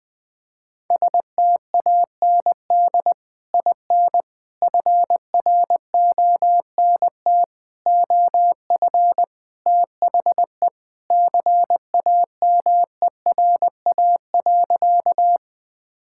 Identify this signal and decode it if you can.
Morse code, heard as a keyed tone, about 20 wpm, 700 Hz. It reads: STAND IN FRONT OF THE CAMERA.
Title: Sentences with 3-Word Phrases - 20wpm